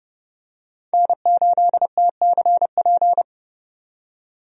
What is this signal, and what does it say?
Morse code, heard as a keyed tone, about 30 wpm, 700 Hz. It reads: N8TCP
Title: USA Call Signs - 5 Characters - 30wpm